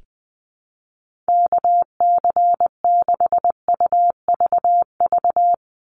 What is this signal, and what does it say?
Morse code, heard as a keyed tone, about 20 wpm, 700 Hz. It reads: KC6UVV
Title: USA Call Signs 20wpm